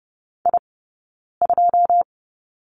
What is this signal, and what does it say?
Morse code, heard as a keyed tone, about 30 wpm, spaced as for 9 wpm, 690 Hz. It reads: I2
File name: course / 30:10wpm / 086 Lesson 15d - Number 5 - Two Characters 30wpm